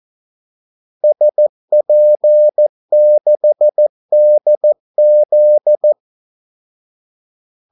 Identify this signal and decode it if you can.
Morse code, heard as a keyed tone, about 14 wpm, 600 Hz. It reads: SP6DZ
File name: Callsigns-14wpm